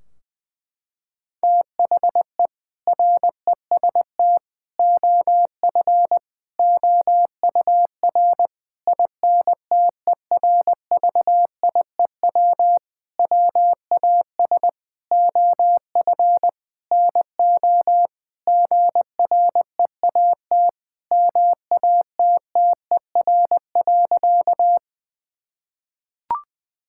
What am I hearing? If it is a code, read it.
Morse code, heard as a keyed tone, about 20 wpm, 700 Hz. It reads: THE REST OF OUR INTERVIEW WAS OF NO GREAT MATTER.
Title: Sentences from Top 1600 Words 20wpm